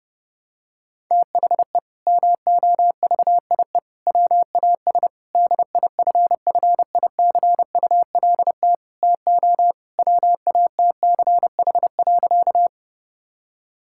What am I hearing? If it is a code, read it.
Morse code, heard as a keyed tone, about 30 wpm, 700 Hz. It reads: THE MOVIE WAS DIFFICULT TO WATCH.
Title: Sentences from Top 1000 Words - Review Part 5 - 30wpm